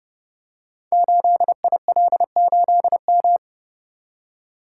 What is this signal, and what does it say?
Morse code, heard as a keyed tone, about 30 wpm, 700 Hz. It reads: 8IL8M
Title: Code Groups - 5 Letter:Numbers - 30wpm